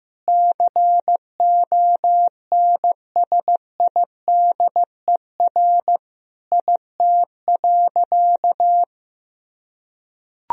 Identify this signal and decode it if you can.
Morse code, heard as a keyed tone, about 15 wpm, 700 Hz. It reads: CONSIDER IT.
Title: Sentences from Top 1000 Words - Review Part 4 - 15wpm